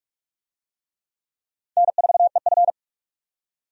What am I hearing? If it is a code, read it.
Morse code, heard as a keyed tone, about 46 wpm, 700 Hz. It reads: N4EF
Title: USA Call Signs - 4 Characters - 45wpm